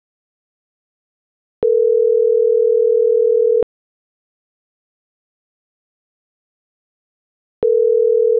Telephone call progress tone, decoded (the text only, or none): ringback tone